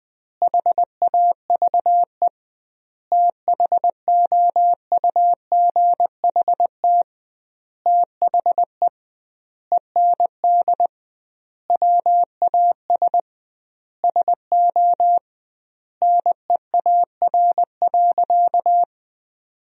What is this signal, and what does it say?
Morse code, heard as a keyed tone, about 20 wpm, 700 Hz. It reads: HAVE THOUGHT THE END WAS SO NEAR.
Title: 2x Word-Spacing - Sentences from Top 300 Words - 20wpm